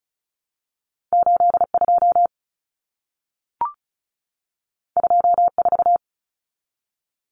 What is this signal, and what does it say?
Morse code, heard as a keyed tone, about 36 wpm, 700 Hz. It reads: 82 24
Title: Common Cut Numbers - 2-Digit 35wpm